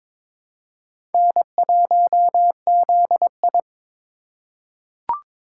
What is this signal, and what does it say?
Morse code, heard as a keyed tone, about 22 wpm, 700 Hz. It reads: N1ZI
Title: USA Call Signs - 4 Characters - 2x Character Spacing - 22wpm